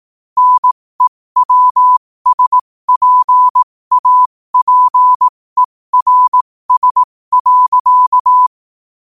Morse code, heard as a keyed tone, about 18 wpm, spaced as for 15 wpm, 1000 Hz. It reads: NEWSPAPERS.
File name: qotdfiles / 15wpm